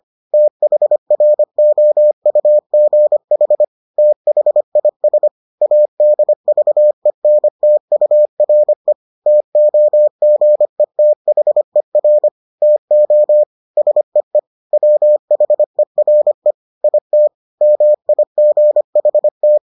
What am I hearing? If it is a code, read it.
Morse code, heard as a keyed tone, about 25 wpm, 600 Hz. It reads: THROUGH THIS ADVENTURE TOGETHER TO SEE WHERE IT MIGHT